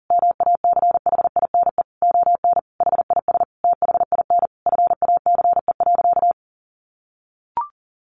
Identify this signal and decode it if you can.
Morse code, heard as a keyed tone, about 40 wpm, 700 Hz. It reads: MACHINE ON HIS THIN FACE.
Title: A Sound Of Thunder 40wpm